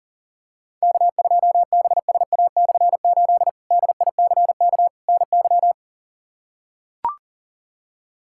Morse code, heard as a keyed tone, about 37 wpm, 700 Hz. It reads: K2BSA/8 DICK NY
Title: K1USN SST QSO Practice - 40wpm